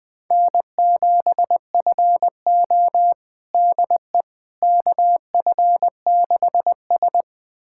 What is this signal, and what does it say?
Morse code, heard as a keyed tone, about 20 wpm, 700 Hz. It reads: N7FO DE KF6S
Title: USA Callsign DE Callsign - 4 Characters - 20wpm